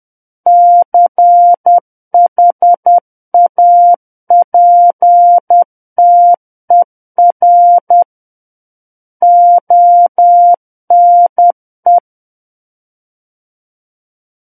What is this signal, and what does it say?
Morse code, heard as a keyed tone, about 10 wpm, 700 Hz. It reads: CHAPTER ONE
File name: Book One / 01-Chapter 1 - 10 WPM